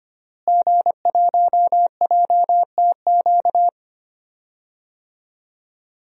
Morse code, heard as a keyed tone, about 25 wpm, 700 Hz. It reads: G1JTQ
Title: DXCC-100 Call Signs - 5 Characters - Repeated 3x - 25wpm